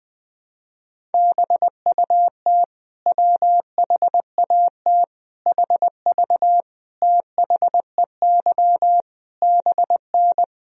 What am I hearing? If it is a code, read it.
Morse code, heard as a keyed tone, about 20 wpm, 700 Hz. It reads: BUT WHAT HV THEY BN
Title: Sentences from Top 100 Words and Abbreviations 20wpm